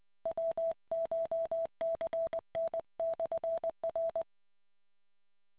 Morse code, heard as a keyed tone, about 20 wpm, 660 Hz. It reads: WCHCN/R